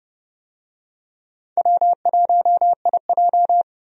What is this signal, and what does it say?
Morse code, heard as a keyed tone, about 30 wpm, 700 Hz. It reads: W1IJ
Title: USA Call Signs - 4 Characters - 30wpm